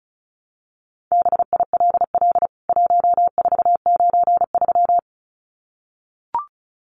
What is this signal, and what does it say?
Morse code, heard as a keyed tone, about 34 wpm, 700 Hz. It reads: BILL 1493
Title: CWT Exchange 35wpm